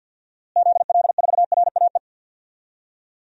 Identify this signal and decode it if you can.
Morse code, heard as a keyed tone, about 50 wpm, 700 Hz. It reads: ZL4RAE
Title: DXCC-100 Call Signs - 6 Characters - Repeated 3x - 50wpm